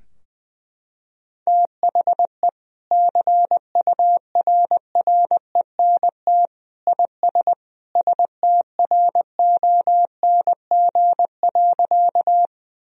Morse code, heard as a keyed tone, about 20 wpm, 700 Hz. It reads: THE CURRENT IS STRONG.
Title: Sentences from Top 1000 Words - Review Part 4 - 20wpm